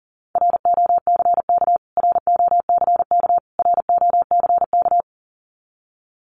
Morse code, heard as a keyed tone, about 40 wpm, 700 Hz. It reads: ROCK ROCK ROCK
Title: Top 500 Words Repeated 40wpm